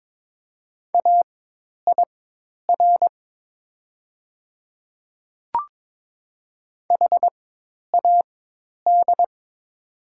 Morse code, heard as a keyed tone, about 22 wpm, spaced as for 10 wpm, 700 Hz. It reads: AIR HAD
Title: 4x Character Spacing - Top 200 Words - 22wpm